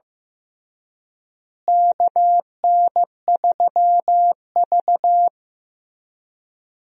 Morse code, heard as a keyed tone, about 15 wpm, 700 Hz. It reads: KN3V
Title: USA Call Signs - 4 Characters - Repeated 3x - 15wpm